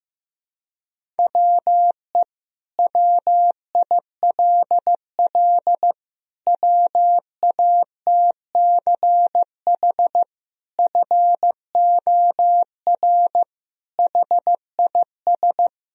Morse code, heard as a keyed tone, about 15 wpm, 700 Hz. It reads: WE WILL WATCH FOR HIS
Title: Sentences from Top 1000 Words - Review Part 1 - 15wpm